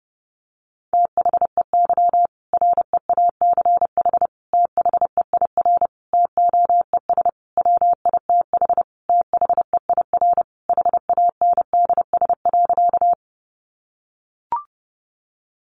Morse code, heard as a keyed tone, about 30 wpm, 700 Hz. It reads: THEY REACH THEIR TOES WITH THEIR HANDS.
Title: Easy Phrases 30wpm